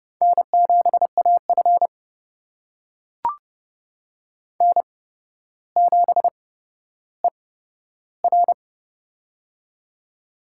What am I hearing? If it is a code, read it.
Morse code, heard as a keyed tone, about 30 wpm, spaced as for 13 wpm, 700 Hz. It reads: N7AF N7ER
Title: USA Call Signs - 4 Characters - 8x Character Spacing - 30wpm